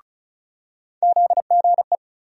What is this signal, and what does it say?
Morse code, heard as a keyed tone, about 36 wpm, 700 Hz. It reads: ZGE
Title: USA Call Sign Postfixes - 3 Letters 35wpm